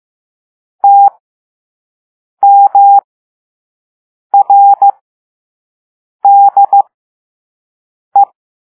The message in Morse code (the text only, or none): TMRDE